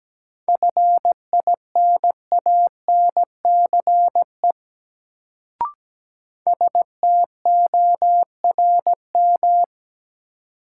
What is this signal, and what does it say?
Morse code, heard as a keyed tone, about 17 wpm, 700 Hz. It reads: FINANCE STORM
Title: Sentences from Top 1500 Words - New Word Drill - 17wpm